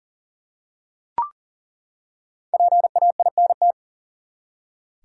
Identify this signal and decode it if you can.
Morse code, heard as a keyed tone, about 40 wpm, 700 Hz. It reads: PAINT
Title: Five Letter Words - 40wpm